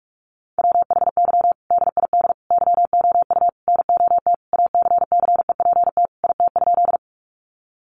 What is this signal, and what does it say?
Morse code, heard as a keyed tone, about 44 wpm, 700 Hz. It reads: WHY DID YOU NOT ACCEPT IT?
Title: Sentences from Top 1300 Words 45wpm